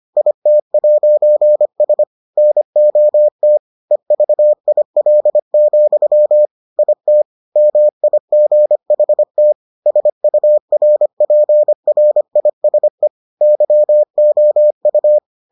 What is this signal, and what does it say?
Morse code, heard as a keyed tone, about 25 wpm, 600 Hz. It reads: IT'S NOT EVIL, IT MIGHT SURPRISE YOU